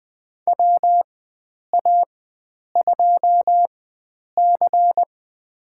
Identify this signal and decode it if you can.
Morse code, heard as a keyed tone, about 20 wpm, spaced as for 9 wpm, 700 Hz. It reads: WA2C
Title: USA Call Signs - 4 Characters - 4x Character Spacing - 20wpm